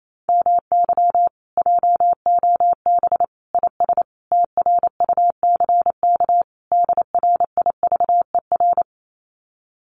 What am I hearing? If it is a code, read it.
Morse code, heard as a keyed tone, about 28 wpm, 700 Hz. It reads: MY JOB IS TRUCK DRIVER
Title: Common QSO Phrases Encore 28wpm